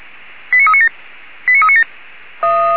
Five-tone selective call (ZVEI), heard as quick runs of five tones, tones of about 70 ms. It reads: 89397 89397
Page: /464/SDR/ZVEI/89391/